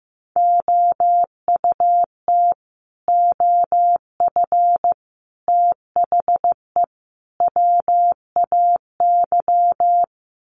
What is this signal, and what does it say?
Morse code, heard as a keyed tone, about 15 wpm, 700 Hz. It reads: OUT OF THE WAY
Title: Sets of 4 Words 15wpm